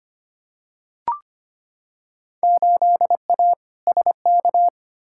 Morse code, heard as a keyed tone, about 25 wpm, 700 Hz. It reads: 8A SK
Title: Field Day Exchanges Encore - 25wpm